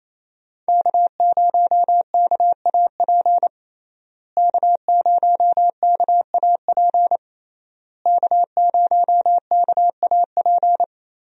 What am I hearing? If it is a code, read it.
Morse code, heard as a keyed tone, about 28 wpm, 700 Hz. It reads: K0KAP K0KAP K0KAP
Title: USA Call Signs - 5 Characters - Repeated 3x - 28wpm